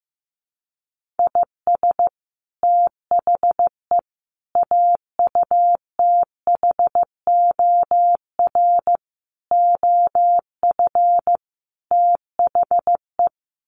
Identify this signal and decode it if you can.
Morse code, heard as a keyed tone, about 15 wpm, 700 Hz. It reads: IS THE AUTHOR OF THE
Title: Sets of 5 Words Encore 15wpm